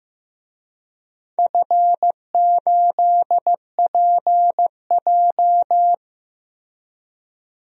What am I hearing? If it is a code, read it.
Morse code, heard as a keyed tone, about 15 wpm, 700 Hz. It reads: F8PJ
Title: DXCC-100 Call Signs - 4 Characters - Repeated 2x - 15wpm